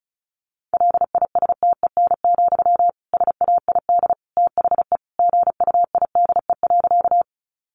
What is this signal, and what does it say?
Morse code, heard as a keyed tone, about 34 wpm, 700 Hz. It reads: LISTEN, SAID THE GUIDE.
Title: Sentences from Top 600 Words 35wpm